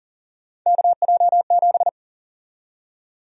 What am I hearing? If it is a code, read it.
Morse code, heard as a keyed tone, about 40 wpm, 700 Hz. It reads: KJ7
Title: USA Call Sign Prefixes - 40wpm